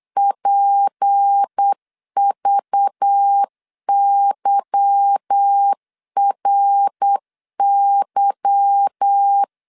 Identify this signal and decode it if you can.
Morse code, heard as a keyed tone, about 8 wpm, 800 Hz. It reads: PVYRY
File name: LEZIONI / 8 YJ45 / 8 Y 8wpm 3245